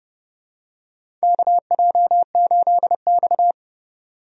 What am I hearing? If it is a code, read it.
Morse code, heard as a keyed tone, about 31 wpm, 700 Hz. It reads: KJ8X
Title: USA Call Signs - 4 Characters - Repeated 3x - 30wpm